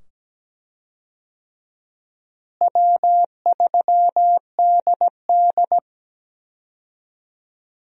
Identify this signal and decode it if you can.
Morse code, heard as a keyed tone, about 17 wpm, 700 Hz. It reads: W3DD
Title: USA Call Signs - 4 Characters - Repeated 2x - 17wpm